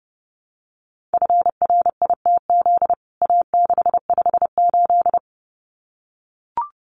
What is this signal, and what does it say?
Morse code, heard as a keyed tone, about 30 wpm, 700 Hz. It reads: FRITZ A658
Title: CWT Exchange 30wpm